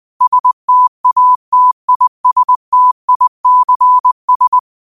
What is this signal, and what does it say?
Morse code, heard as a keyed tone, about 20 wpm, 1000 Hz. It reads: STATISTICS